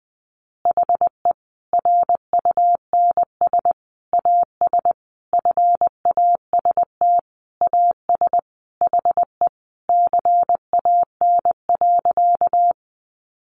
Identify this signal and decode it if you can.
Morse code, heard as a keyed tone, about 20 wpm, 700 Hz. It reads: HE RUNS AS FAST AS HE CAN.
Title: Additional Easy Phrases 20wpm